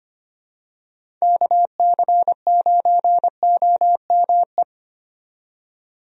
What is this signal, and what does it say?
Morse code, heard as a keyed tone, about 25 wpm, 700 Hz. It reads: KC9OME
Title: USA Call Signs - 6 Characters - 25wpm